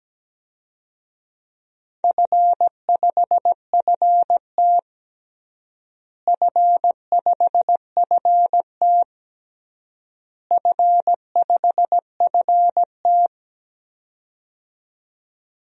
Morse code, heard as a keyed tone, about 17 wpm, 700 Hz. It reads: F5FT F5FT F5FT
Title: DXCC-100 Call Signs - 4 Characters - Repeated 3x - 17wpm